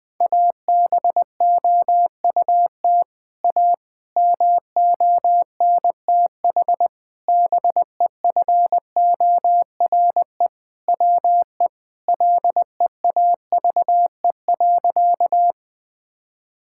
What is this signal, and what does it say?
Morse code, heard as a keyed tone, about 20 wpm, 700 Hz. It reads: ABOUT A MONTH BEFORE WE LEAVE.
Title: Sentences from Top 1000 Words - Review Part 5 - 20wpm